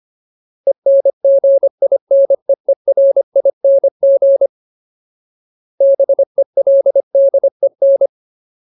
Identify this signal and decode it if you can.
Morse code, heard as a keyed tone, about 25 wpm, 550 Hz. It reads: ENGINEERING BELDEN
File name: Ultimate Word List / 25WPM Ultimate